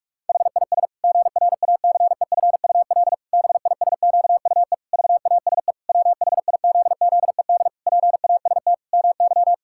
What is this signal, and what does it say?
Morse code, heard as a keyed tone, about 41 wpm, 700 Hz. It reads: HIS GRACEFUL BISQUE VASE WHIZZED PAST MY
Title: Pangram Challenge v2 - 45wpm